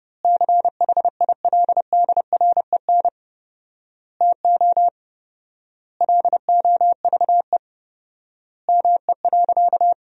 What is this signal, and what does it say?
Morse code, heard as a keyed tone, about 30 wpm, 700 Hz. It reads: CHILDREN TO LOVE ME.
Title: 4x Word-Spacing - Sentences from Top 400 Words - 30wpm